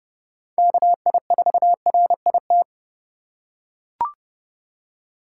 Morse code, heard as a keyed tone, about 26 wpm, 700 Hz. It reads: KI4RIT